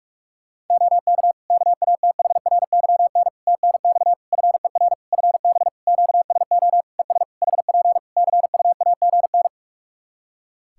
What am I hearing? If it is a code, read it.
Morse code, heard as a keyed tone, about 44 wpm, 700 Hz. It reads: OK KATHRYN TNX FER FB QSO ES HP CUAGN
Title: K1USN SST QSO Practice 45wpm